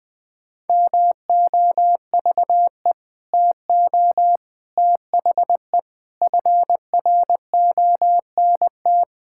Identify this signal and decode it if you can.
Morse code, heard as a keyed tone, about 20 wpm, 700 Hz. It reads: MOVE TO THE FRONT
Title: Sentences with 5-Word Phrases - Spoken Beginning - 20wpm